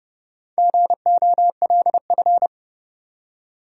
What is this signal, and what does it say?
Morse code, heard as a keyed tone, about 30 wpm, 700 Hz. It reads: GOLF
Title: Four Letter Words - 30wpm